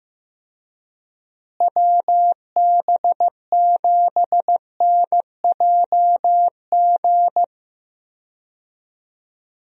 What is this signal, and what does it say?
Morse code, heard as a keyed tone, about 15 wpm, 700 Hz. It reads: WB7NJG
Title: USA Call Signs - 6 Characters - Repeated 3x - 15wpm